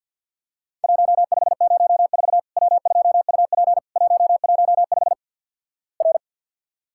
Morse code, heard as a keyed tone, about 50 wpm, 700 Hz. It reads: 1504 W2UP 115 R
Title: Worked All Europe DX Contest - QTC Exchange with Repeats - 50wpm